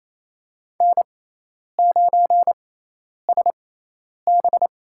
Morse code, heard as a keyed tone, about 28 wpm, spaced as for 10 wpm, 700 Hz. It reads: N9SB
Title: USA Call Signs - 6x Character Spacing - 28wpm